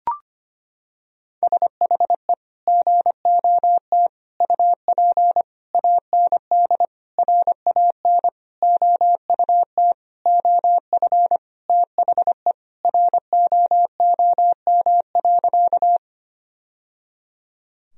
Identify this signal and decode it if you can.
Morse code, heard as a keyed tone, about 25 wpm, 700 Hz. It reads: SHE GOT UP AND RAN OUT OF THE ROOM.
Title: Sentences from Top 1000 Words - Review Part 3 - 25wpm